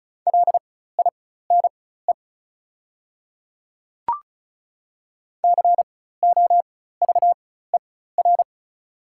Morse code, heard as a keyed tone, about 36 wpm, spaced as for 16 wpm, 690 Hz. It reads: LINE COVER
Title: 4x Character Spacing - Top 300 Words - 35wpm